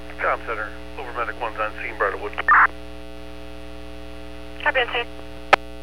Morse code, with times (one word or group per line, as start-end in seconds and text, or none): none